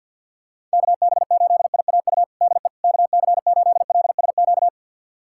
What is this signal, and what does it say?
Morse code, heard as a keyed tone, about 49 wpm, 700 Hz. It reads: KB8IAU DE KC8LSQ
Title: USA Callsign DE Callsign - 6 Characters - 50wpm